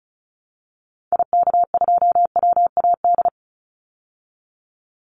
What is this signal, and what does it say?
Morse code, heard as a keyed tone, about 34 wpm, 700 Hz. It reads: IK2WAD
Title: Call Signs Repeated 3x - 35wpm